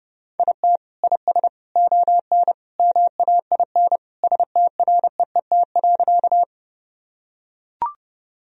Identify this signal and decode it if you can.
Morse code, heard as a keyed tone, about 32 wpm, 700 Hz. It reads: IT IS ON MAIN STREET.
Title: Sentences from Top 1000 Words - Review Part 2 - 30wpm